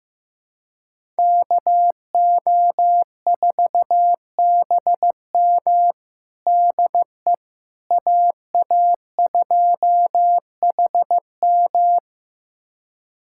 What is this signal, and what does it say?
Morse code, heard as a keyed tone, about 15 wpm, 700 Hz. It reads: KO4BM DE AA2HM
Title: USA Callsign DE Callsign - 5 Characters - 15wpm